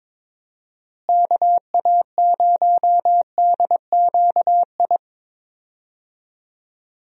Morse code, heard as a keyed tone, about 22 wpm, 700 Hz. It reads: KA0DQI
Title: USA Call Signs - 6 Characters - Repeated 2x - 22wpm